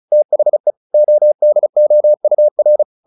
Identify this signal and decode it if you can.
Morse code, heard as a keyed tone, about 35 wpm, 600 Hz. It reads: THE ODOUR
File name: the old man and the sea / the old man and the sea 1 35wpm